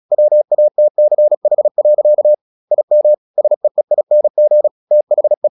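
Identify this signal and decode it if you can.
Morse code, heard as a keyed tone, about 36 wpm, 600 Hz. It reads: WATCH. IM SEEING THE
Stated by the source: Morse practice at 36 wpm, text of Ken Follett's novel Never, from CW med Gnister.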